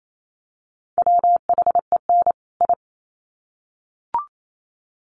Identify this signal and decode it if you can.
Morse code, heard as a keyed tone, about 28 wpm, 700 Hz. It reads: WHEN I
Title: Sets of 2 Words 28wpm